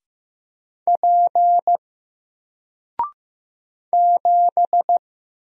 Morse code, heard as a keyed tone, about 15 wpm, 700 Hz. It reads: P 7
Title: Single Character (Letter:Number:Punctuation:Prosign) Alt 15wpm